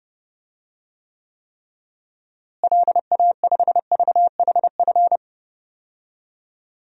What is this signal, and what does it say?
Morse code, heard as a keyed tone, about 30 wpm, 700 Hz. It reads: LA5VHF
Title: DXCC-100 Call Signs - 6 Characters - Repeated 3x - 30wpm